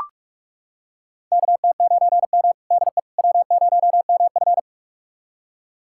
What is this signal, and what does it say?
Morse code, heard as a keyed tone, about 42 wpm, 700 Hz. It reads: KT9M DE W0MF
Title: USA Callsign DE Callsign - 4 Characters - 45wpm